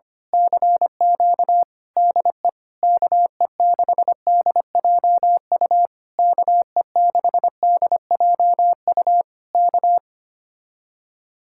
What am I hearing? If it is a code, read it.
Morse code, heard as a keyed tone, about 24 wpm, 700 Hz. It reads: CQ DE KE6DJU KE6DJU K
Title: Speed-Racing Common QSO Phrases - 25wpm